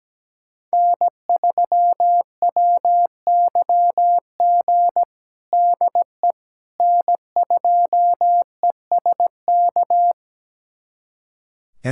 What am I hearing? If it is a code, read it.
Morse code, heard as a keyed tone, about 17 wpm, 700 Hz. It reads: N3WYG DE N2ESK
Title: USA Callsign DE Callsign - 5 Characters - 17wpm